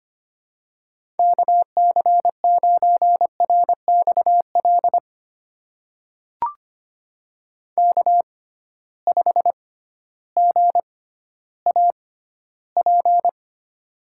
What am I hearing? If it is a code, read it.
Morse code, heard as a keyed tone, about 25 wpm, spaced as for 13 wpm, 700 Hz. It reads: KC9RXL K5GAP
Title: USA Call Signs - 6x Character Spacing - 25wpm